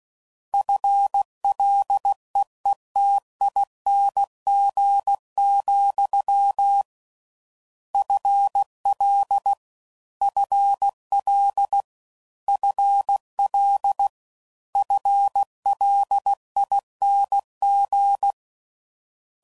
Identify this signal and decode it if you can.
Morse code, heard as a keyed tone, about 16 wpm, 800 Hz. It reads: FLEETING, FL FL FL FLING